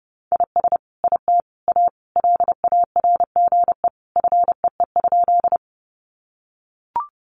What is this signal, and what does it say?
Morse code, heard as a keyed tone, about 30 wpm, 700 Hz. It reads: IS IT A LARGE FEE?
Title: Sentences from Top 1500 Words 30wpm